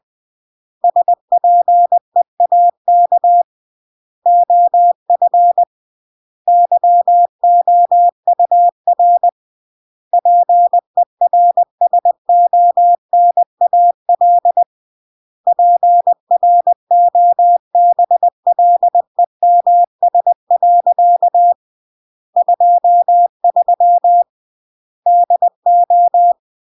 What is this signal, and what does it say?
Morse code, heard as a keyed tone, about 20 wpm, 700 Hz. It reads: SPEAK OF YOUR PERSONAL PROBLEMS. 23 DO
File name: CW / jodorowsky / 82 commandments 20-20